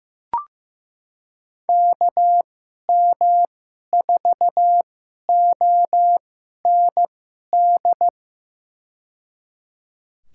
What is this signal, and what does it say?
Morse code, heard as a keyed tone, about 15 wpm, spaced as for 11 wpm, 700 Hz. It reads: KM4OND
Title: USA Call Signs - 6 Characters - 2x Character Spacing - 15wpm